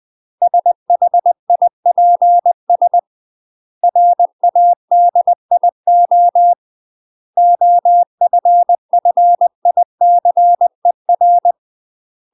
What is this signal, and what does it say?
Morse code, heard as a keyed tone, about 20 wpm, 700 Hz. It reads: SHIPS RADIO OFFICER
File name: UK MASTER QSOs TRAINING / 20WPM / UK MASTER QSOs 0037